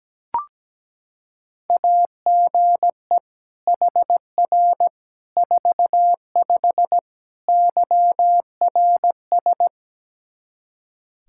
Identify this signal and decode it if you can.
Morse code, heard as a keyed tone, about 17 wpm, 700 Hz. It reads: AGE HR 45 YRS
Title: Practice QSO - Exchange 3 - More Info - 17wpm